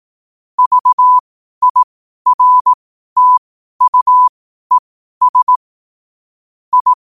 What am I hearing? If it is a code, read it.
Morse code, heard as a keyed tone, about 18 wpm, spaced as for 12 wpm, 1000 Hz. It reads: VIRTUES I
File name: qotdfiles / 12wpm